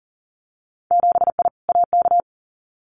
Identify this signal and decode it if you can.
Morse code, heard as a keyed tone, about 40 wpm, 700 Hz. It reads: 7I AK
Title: Winter Field Day Exchanges Encore 40wpm